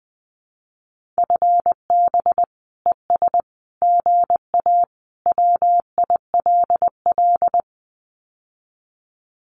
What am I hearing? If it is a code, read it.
Morse code, heard as a keyed tone, about 20 wpm, 700 Hz. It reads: FB ES GA WILL
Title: Common QSO Phrases Encore 20wpm